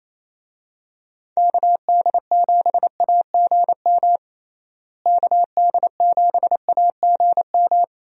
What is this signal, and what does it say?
Morse code, heard as a keyed tone, about 27 wpm, 700 Hz. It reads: KD7AGM KD7AGM
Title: USA Call Signs - 6 Characters - Repeated 2x - 28wpm